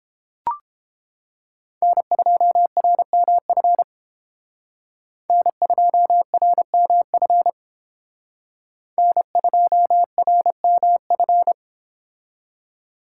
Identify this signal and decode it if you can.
Morse code, heard as a keyed tone, about 31 wpm, 700 Hz. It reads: N2RMF N2RMF N2RMF